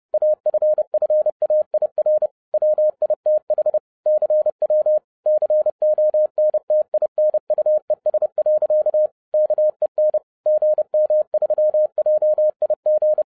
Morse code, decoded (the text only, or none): AFFAIR WITH CW CONTINUES. KEN GM3JIG